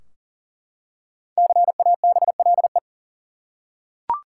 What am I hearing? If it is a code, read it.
Morse code, heard as a keyed tone, about 40 wpm, 700 Hz. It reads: CABLE
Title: Five Letter Words - 40wpm